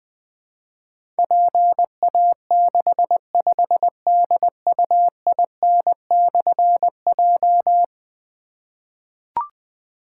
Morse code, heard as a keyed tone, about 20 wpm, 700 Hz. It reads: PA65DUIN/J
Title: Difficult Call Signs 20wpm